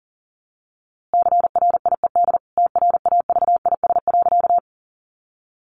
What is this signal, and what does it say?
Morse code, heard as a keyed tone, about 40 wpm, 700 Hz. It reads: CRIED TRAVIS.
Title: A Sound Of Thunder 40wpm